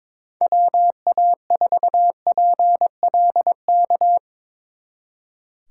Morse code, heard as a keyed tone, about 24 wpm, 700 Hz. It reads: WA4PLK
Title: Speed Racing - USA 2x3 Call Signs 22wpm